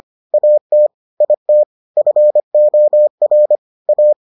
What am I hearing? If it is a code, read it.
Morse code, heard as a keyed tone, about 25 wpm, 600 Hz. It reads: AT IT FOR A